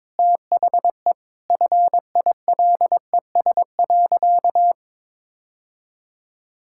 Morse code, heard as a keyed tone, about 22 wpm, 700 Hz. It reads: THE FILES.
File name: practice / 22wpm / Easy Sentences with Hints - 22wpm